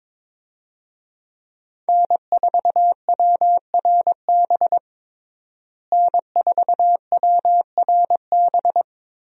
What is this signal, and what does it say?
Morse code, heard as a keyed tone, about 22 wpm, 700 Hz. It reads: N4WRB N4WRB
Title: USA Call Signs - 5 Characters - Repeated 3x - 22wpm